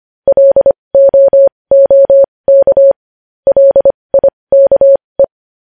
Morse code, heard as a keyed tone, about 25 wpm, spaced as for 20 wpm, 550 Hz. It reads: LOOK LIKE
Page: news-headlines-in-morse-code-at-20-wpm